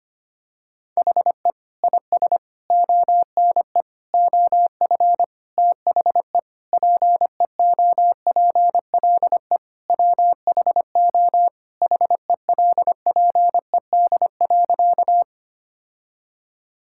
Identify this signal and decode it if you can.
Morse code, heard as a keyed tone, about 25 wpm, 700 Hz. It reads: HE IS ONE OF THE PEOPLE WHO HELPED.
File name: practice / 25wpm / Sentences with 4-Word Phrases - 25wpm